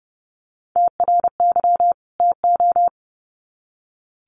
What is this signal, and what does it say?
Morse code, heard as a keyed tone, about 30 wpm, 700 Hz. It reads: TRY TO
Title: Sets of 2 Words 30wpm